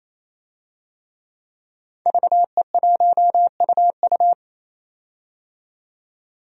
Morse code, heard as a keyed tone, about 28 wpm, 700 Hz. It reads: VE1UU
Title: DXCC-100 Call Signs - 5 Characters - 28wpm